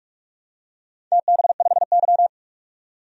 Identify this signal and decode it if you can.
Morse code, heard as a keyed tone, about 42 wpm, 700 Hz. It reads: TB5Y
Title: DXCC-100 Call Signs - 4 Characters - 45wpm